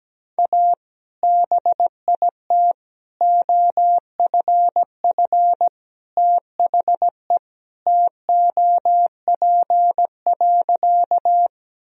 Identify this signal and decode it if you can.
Morse code, heard as a keyed tone, about 17 wpm, 700 Hz. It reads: A BIT OFF THE TOP.
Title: Sentences from Top 1000 Words - Review Part 2 - 17wpm